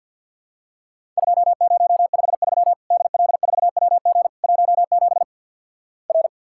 Morse code, heard as a keyed tone, about 50 wpm, 700 Hz. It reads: 2053 DL4WG 17 R
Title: Worked All Europe DX Contest - QTC Exchange - 50wpm